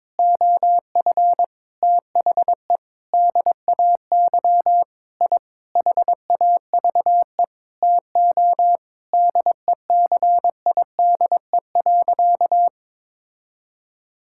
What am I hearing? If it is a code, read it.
Morse code, heard as a keyed tone, about 22 wpm, 700 Hz. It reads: OF THE DAY I HAVE TO DECIDE.
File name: practice / 22wpm / Sentences with 8-Word Phrases - 22wpm